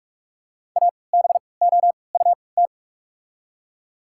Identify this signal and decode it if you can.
Morse code, heard as a keyed tone, about 44 wpm, spaced as for 25 wpm, 690 Hz. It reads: ABOUT
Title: 3x Character Spacing - Top 100 Words - 45wpm